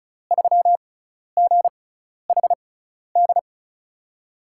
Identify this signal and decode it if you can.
Morse code, heard as a keyed tone, about 35 wpm, spaced as for 12 wpm, 700 Hz. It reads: 3GHD
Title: USA Call Signs - 6x Character Spacing - 35wpm